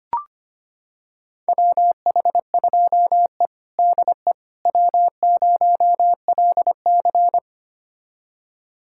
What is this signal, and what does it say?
Morse code, heard as a keyed tone, about 25 wpm, 700 Hz. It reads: WH2E DE W0LC
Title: USA Callsign DE Callsign - 4 Characters - 25wpm